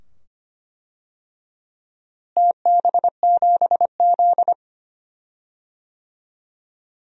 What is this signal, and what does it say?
Morse code, heard as a keyed tone, about 25 wpm, 700 Hz. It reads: TB7Z
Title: DXCC-100 Call Signs - 4 Characters - Repeated 2x - 25wpm